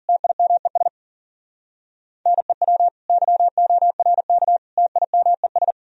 Morse code, heard as a keyed tone, about 50 wpm, 700 Hz. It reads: TIMES NEW YORK TIMES